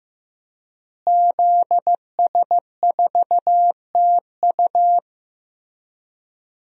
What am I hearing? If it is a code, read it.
Morse code, heard as a keyed tone, about 15 wpm, 700 Hz. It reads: ZS4TU
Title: DXCC-100 Call Signs - 5 Characters - Repeated 3x - 15wpm